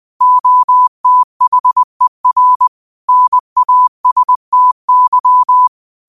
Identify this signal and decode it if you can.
Morse code, heard as a keyed tone, about 20 wpm, 1000 Hz. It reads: OTHER NASTY
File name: qotdfiles / 20wpm